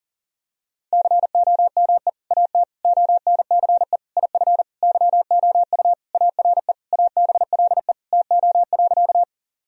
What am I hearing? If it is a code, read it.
Morse code, heard as a keyed tone, about 40 wpm, 700 Hz. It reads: COME AT ONCE IF YOU ARE ABLE TO.
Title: Sentences from Top 1000 Words - Review Part 2 - 40wpm